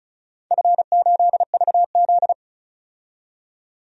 Morse code, heard as a keyed tone, about 34 wpm, 700 Hz. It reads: F8VZ
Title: DXCC-100 Call Signs - 4 Characters - 35wpm